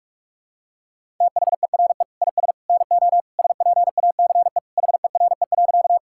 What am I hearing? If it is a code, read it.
Morse code, heard as a keyed tone, about 47 wpm, 700 Hz. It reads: THERE IS NO SPACE HERE.
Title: Sentences from Top 1000 Words - Review Part 5 - 45wpm